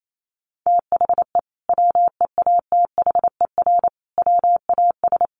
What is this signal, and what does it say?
Morse code, heard as a keyed tone, about 28 wpm, 700 Hz. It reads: THE WEATHER WAS
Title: Sentences from Top 1400 Words 28wpm